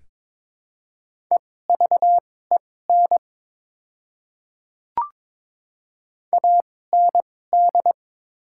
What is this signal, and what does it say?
Morse code, heard as a keyed tone, about 21 wpm, spaced as for 16 wpm, 700 Hz. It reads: EVEN AND
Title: 2x Character Spacing - Top 200 Words - 22wpm